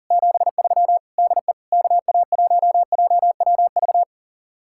Morse code, heard as a keyed tone, about 49 wpm, 700 Hz. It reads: 73 DE KA1JWV
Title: Speed-Racing Common QSO Phrases - 40wpm